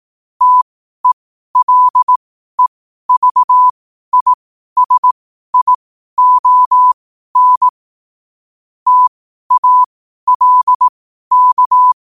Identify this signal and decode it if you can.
Morse code, heard as a keyed tone, about 18 wpm, spaced as for 12 wpm, 1000 Hz. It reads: TELEVISION TALK
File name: qotdfiles / 12wpm